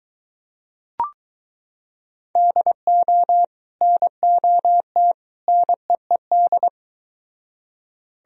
Morse code, heard as a keyed tone, about 22 wpm, 700 Hz. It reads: DO NOT NEED